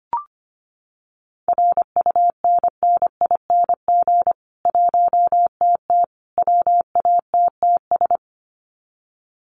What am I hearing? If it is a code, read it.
Morse code, heard as a keyed tone, about 25 wpm, 700 Hz. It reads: RUNNING 1TT WATTS
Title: Common QSO Phrases 25wpm